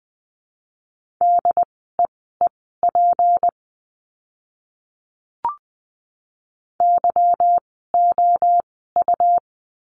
Morse code, heard as a keyed tone, about 20 wpm, spaced as for 14 wpm, 700 Hz. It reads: DEEP YOU
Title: 153 Lesson 25f - Letter p - Words 20wpm